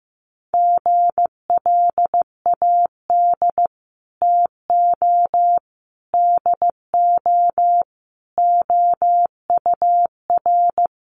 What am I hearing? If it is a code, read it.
Morse code, heard as a keyed tone, about 15 wpm, 700 Hz. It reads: GLAD TO DO OUR
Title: Sentences from Top 800 Words 15wpm